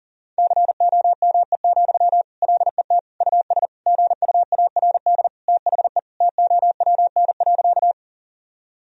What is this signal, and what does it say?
Morse code, heard as a keyed tone, about 40 wpm, 700 Hz. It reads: COME, LET US GUARD THE TOWN.
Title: Sentences from Top 1000 Words - Review Part 4 - 40wpm